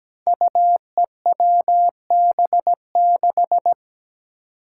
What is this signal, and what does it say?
Morse code, heard as a keyed tone, about 17 wpm, 700 Hz. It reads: UEWB6
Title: Five Challenging Letter-Number Combinations 17wpm